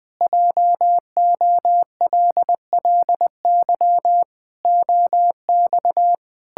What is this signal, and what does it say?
Morse code, heard as a keyed tone, about 20 wpm, 700 Hz. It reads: JOLLY OX
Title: Pangram Challenge v2 - 20wpm